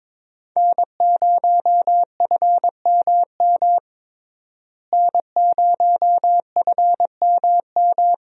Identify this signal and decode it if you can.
Morse code, heard as a keyed tone, about 22 wpm, 700 Hz. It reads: N0FMM N0FMM
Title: USA Call Signs - 5 Characters - Repeated 3x - 22wpm